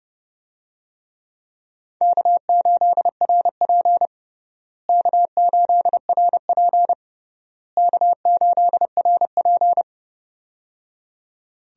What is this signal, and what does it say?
Morse code, heard as a keyed tone, about 30 wpm, 700 Hz. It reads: K8RP K8RP K8RP
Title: USA Call Signs - 4 Characters - Repeated 3x - 30wpm